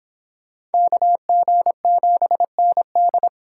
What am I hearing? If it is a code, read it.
Morse code, heard as a keyed tone, about 30 wpm, 700 Hz. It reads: KG7ND